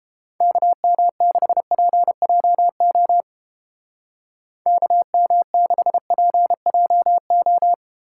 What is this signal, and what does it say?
Morse code, heard as a keyed tone, about 30 wpm, 700 Hz. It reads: KM6PJO KM6PJO